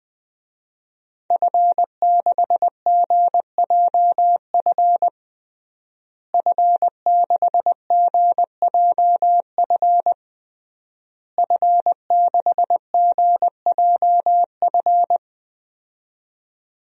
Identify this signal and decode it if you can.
Morse code, heard as a keyed tone, about 21 wpm, 700 Hz. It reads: F6GJF F6GJF F6GJF
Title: DXCC-100 Call Signs - 5 Characters - Repeated 3x - 20wpm